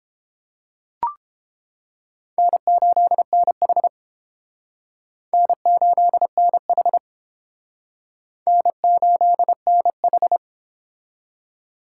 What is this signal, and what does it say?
Morse code, heard as a keyed tone, about 30 wpm, 700 Hz. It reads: N8NH N8NH N8NH